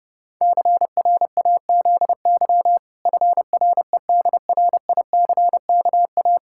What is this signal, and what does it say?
Morse code, heard as a keyed tone, about 32 wpm, 700 Hz. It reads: CRAZY FREDRICKA
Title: Pangram Challenge v2 - 30wpm